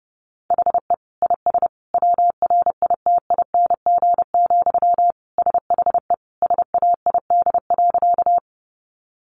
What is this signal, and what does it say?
Morse code, heard as a keyed tone, about 30 wpm, 700 Hz. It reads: HE IS WRITING, SHE SAID.
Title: Sentences from Top 800 Words 30wpm